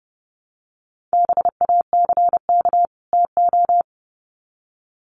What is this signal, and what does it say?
Morse code, heard as a keyed tone, about 30 wpm, 700 Hz. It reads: BACK TO
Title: Sets of 2 Words 30wpm